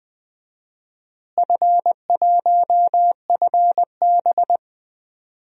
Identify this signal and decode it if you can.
Morse code, heard as a keyed tone, about 20 wpm, 700 Hz. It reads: F1FB
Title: DXCC-100 Call Signs - 4 Characters - Repeated 3x - 20wpm